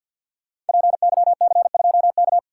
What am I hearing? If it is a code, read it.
Morse code, heard as a keyed tone, about 53 wpm, 700 Hz. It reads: PYC2K
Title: Code Groups - 5 Letter:Numbers - 50wpm